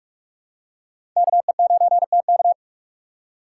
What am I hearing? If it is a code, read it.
Morse code, heard as a keyed tone, about 46 wpm, 700 Hz. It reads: KE9TK
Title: USA Call Signs - 5 Characters - 4x Character Spacing - 45wpm